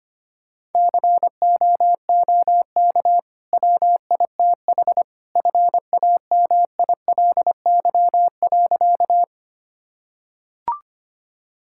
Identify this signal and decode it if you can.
Morse code, heard as a keyed tone, about 25 wpm, 700 Hz. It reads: COOK WITH FAMILY.
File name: Easy Sentences with Hints - 25wpm